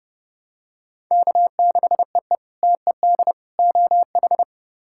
Morse code, heard as a keyed tone, about 31 wpm, 700 Hz. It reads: K6EE TED OH
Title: K1USN SST QSO Practice - 30wpm